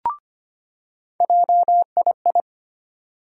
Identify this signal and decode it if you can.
Morse code, heard as a keyed tone, about 25 wpm, 700 Hz. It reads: JII
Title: USA Call Sign Postfixes - 3 Letters 25wpm